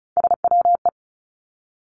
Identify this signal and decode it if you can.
Morse code, heard as a keyed tone, about 36 wpm, 700 Hz. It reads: SWE
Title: Three Challenging Letter-Number Combinations 35wpm